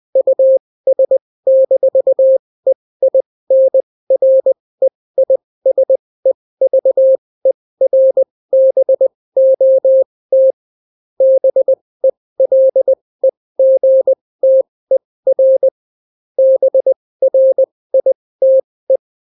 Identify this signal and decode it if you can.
Morse code, heard as a keyed tone, about 20 wpm, spaced as for 16 wpm, 530 Hz. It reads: US-EINREISEVERBOT BELEGTER BRITE